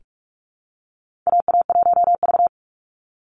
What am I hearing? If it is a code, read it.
Morse code, heard as a keyed tone, about 44 wpm, 700 Hz. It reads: AA1V
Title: Call Signs - 45wpm